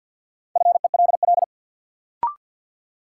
Morse code, heard as a keyed tone, about 52 wpm, 700 Hz. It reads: FELL
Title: Sentences from Top 1600 Words - New Word Drill Alt - 50wpm